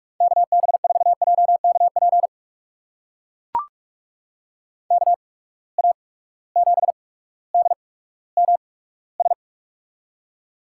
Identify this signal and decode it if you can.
Morse code, heard as a keyed tone, about 44 wpm, spaced as for 19 wpm, 700 Hz. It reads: KB4JKP KA7DMS
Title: USA Call Signs - 6 Characters - 8x Character Spacing - 45wpm